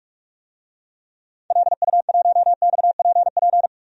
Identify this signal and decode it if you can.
Morse code, heard as a keyed tone, about 43 wpm, 700 Hz. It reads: LU1XPP
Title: DXCC-100 Call Signs - 6 Characters - 45wpm